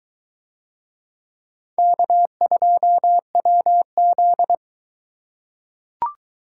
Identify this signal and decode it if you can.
Morse code, heard as a keyed tone, about 20 wpm, 700 Hz. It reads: K2WZ